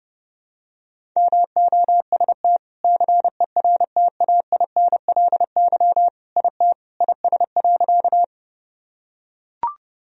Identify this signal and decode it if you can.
Morse code, heard as a keyed tone, about 28 wpm, 700 Hz. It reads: MOST CERTAINLY IT IS.
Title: Sentences from Top 1000 Words - Review Part 1 - 30wpm